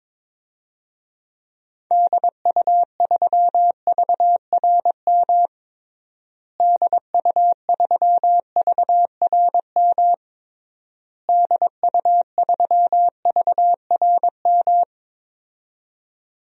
Morse code, heard as a keyed tone, about 22 wpm, 700 Hz. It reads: DU3VRM DU3VRM DU3VRM
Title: DXCC-100 Call Signs - 6 Characters - Repeated 3x - 22wpm